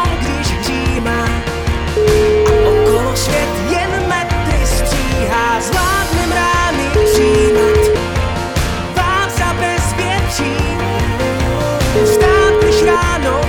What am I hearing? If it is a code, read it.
Ringback tone (Europe, 425 Hz), with noise.